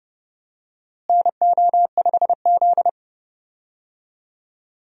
Morse code, heard as a keyed tone, about 31 wpm, 700 Hz. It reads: NO5Z
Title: USA Call Signs - 4 Characters - Repeated 3x - 30wpm